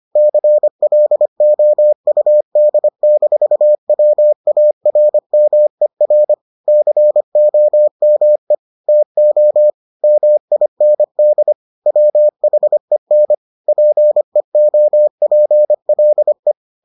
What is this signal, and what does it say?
Morse code, heard as a keyed tone, about 25 wpm, 600 Hz. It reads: CLOUD-WARMER COME TO MIND WHEN PEOPLE